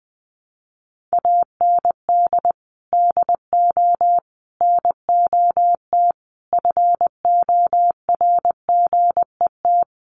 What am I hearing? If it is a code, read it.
Morse code, heard as a keyed tone, about 20 wpm, 700 Hz. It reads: AND DO NOT FORGET
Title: Sentences from Top 1200 Words 20wpm